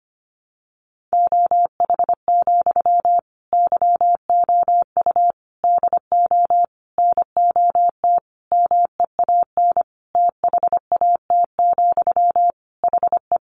Morse code, heard as a keyed tone, about 25 wpm, 700 Hz. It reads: OH, YOU DO NOT MEAN THAT, HE
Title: Sentences from Top 600 Words 25wpm